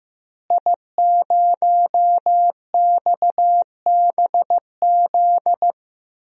Morse code, heard as a keyed tone, about 15 wpm, 700 Hz. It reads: I0XBZ
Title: DXCC-100 Call Signs - 5 Characters - Repeated 2x - 15wpm